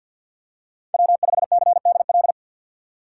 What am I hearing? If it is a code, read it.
Morse code, heard as a keyed tone, about 52 wpm, 700 Hz. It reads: W5CDL
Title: USA Call Signs - 5 Characters - 2x Character Spacing - 50wpm